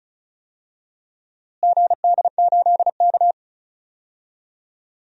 Morse code, heard as a keyed tone, about 34 wpm, 700 Hz. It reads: GD8K
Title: DXCC-100 Call Signs - 4 Characters - Repeated 2x - 35wpm